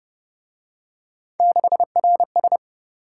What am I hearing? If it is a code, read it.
Morse code, heard as a keyed tone, about 30 wpm, 700 Hz. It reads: 6RS
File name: Speed-Racing - Three Letter-Number Combinations - 25wpm